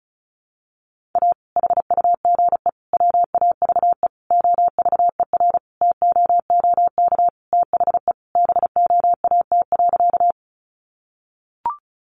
Morse code, heard as a keyed tone, about 34 wpm, 700 Hz. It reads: A HUGE WAVE OVER TOOK THE BOAT.
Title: Sentences from Top 1400 Words 35wpm